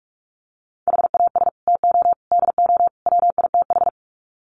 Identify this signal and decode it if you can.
Morse code, heard as a keyed tone, about 44 wpm, 700 Hz. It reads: HAS TO DO WITH
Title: Sets of 4 Words 45wpm